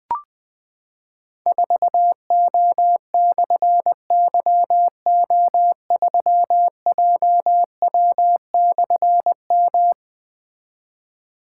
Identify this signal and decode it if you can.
Morse code, heard as a keyed tone, about 20 wpm, 700 Hz. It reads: 4O/YO3JW/M
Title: Difficult Call Signs 20wpm